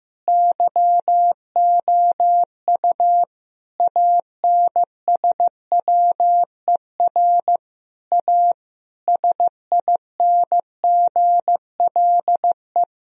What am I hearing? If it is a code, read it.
Morse code, heard as a keyed tone, about 15 wpm, 700 Hz. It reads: YOU ANSWER A SINGLE
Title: Sentences from Top 1000 Words - Review Part 4 - 15wpm